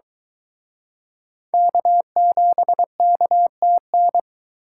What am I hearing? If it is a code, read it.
Morse code, heard as a keyed tone, about 23 wpm, 700 Hz. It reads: K7KTN